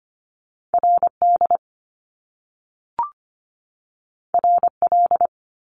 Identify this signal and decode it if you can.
Morse code, heard as a keyed tone, about 25 wpm, 700 Hz. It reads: RD RL
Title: Top Two Letter Combinations in Words 25wpm